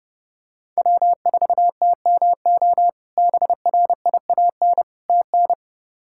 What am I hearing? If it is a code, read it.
Morse code, heard as a keyed tone, about 30 wpm, 700 Hz. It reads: W4TMO BRIAN TN
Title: K1USN SST QSO Practice - 30wpm